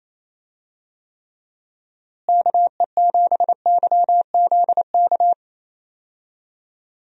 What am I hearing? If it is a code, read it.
Morse code, heard as a keyed tone, about 28 wpm, 700 Hz. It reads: KE7YZK
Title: USA Call Signs - 6 Characters - Repeated 2x - 28wpm